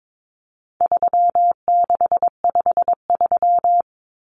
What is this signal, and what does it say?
Morse code, heard as a keyed tone, about 22 wpm, 700 Hz. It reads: 3653
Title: Common Cut Numbers - 4-Digit 22wpm